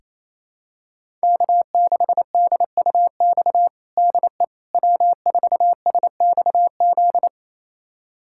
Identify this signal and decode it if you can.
Morse code, heard as a keyed tone, about 28 wpm, 700 Hz. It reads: K6DUX DE W4SXZ
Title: USA Callsign DE Callsign - 5 Characters - 28wpm